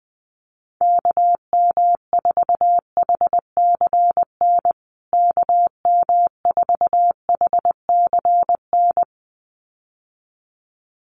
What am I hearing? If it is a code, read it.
Morse code, heard as a keyed tone, about 20 wpm, 700 Hz. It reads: KM4HCN KM4HCN
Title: USA Call Signs Repeated 2x 20wpm